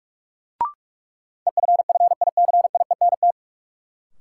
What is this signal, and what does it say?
Morse code, heard as a keyed tone, about 46 wpm, 700 Hz. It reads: EFFICIENT
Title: Sentences from Top 1300 Words - New Word Drill Alt - 45wpm